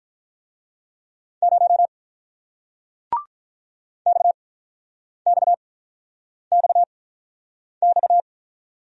Morse code, heard as a keyed tone, about 48 wpm, spaced as for 9 wpm, 700 Hz. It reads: 0 XXXX